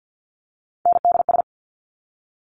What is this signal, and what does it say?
Morse code, heard as a keyed tone, about 50 wpm, 700 Hz. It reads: NDS
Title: Top Three Letter Combinations in Words 50wpm